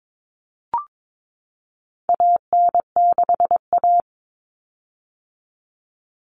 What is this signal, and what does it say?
Morse code, heard as a keyed tone, about 22 wpm, 700 Hz. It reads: AN6A
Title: Common Cut Numbers - 4-Digit 22wpm